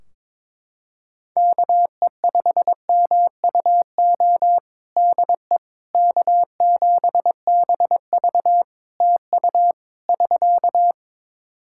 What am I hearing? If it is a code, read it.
Morse code, heard as a keyed tone, about 22 wpm, 700 Hz. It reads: KE5MUO DE K7BV TU <SK>
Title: K1USN SST QSO Practice 22wpm